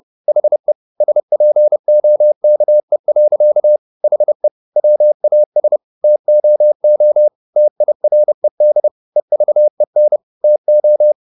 Morse code, heard as a keyed tone, about 30 wpm, 600 Hz. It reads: HE SPOKE. HE WAS TOO TIRED EVEN TO
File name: the old man and the sea / the old man and the sea 1 30wpm